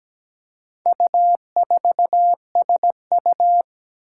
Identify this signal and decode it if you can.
Morse code, heard as a keyed tone, about 17 wpm, 700 Hz. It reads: U4SU
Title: Four Challenging Letter-Number Combinations 17wpm